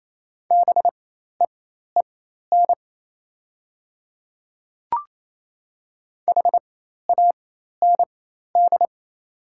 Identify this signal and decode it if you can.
Morse code, heard as a keyed tone, about 27 wpm, spaced as for 13 wpm, 700 Hz. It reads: BEEN HAND
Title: 4x Character Spacing - Top 200 Words - 28wpm